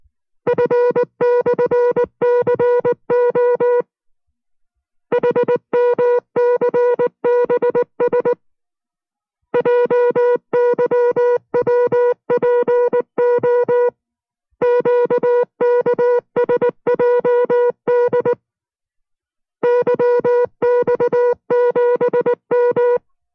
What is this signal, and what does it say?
Morse code, heard as a keyed tone, about 19 wpm, 490 Hz. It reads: F/CO HMCBS JYWPO QKSJD YX7M